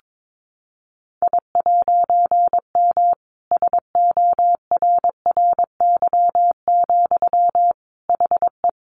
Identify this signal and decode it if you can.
Morse code, heard as a keyed tone, about 22 wpm, 700 Hz. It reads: I'M SORRY, HE
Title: A Sound Of Thunder 22wpm